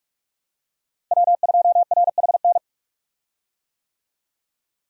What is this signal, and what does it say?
Morse code, heard as a keyed tone, about 46 wpm, 700 Hz. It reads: W2RHN
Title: USA Call Signs - 5 Characters - Repeated 2x - 45wpm